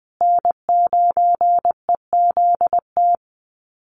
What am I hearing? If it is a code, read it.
Morse code, heard as a keyed tone, about 20 wpm, 700 Hz. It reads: N9EZT
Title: USA Call Signs Repeated 3x 20wpm